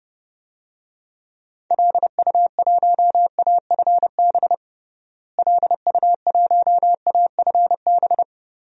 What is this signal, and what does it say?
Morse code, heard as a keyed tone, about 30 wpm, 700 Hz. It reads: LU1AFB LU1AFB
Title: DXCC-100 Call Signs - 6 Characters - Repeated 3x - 30wpm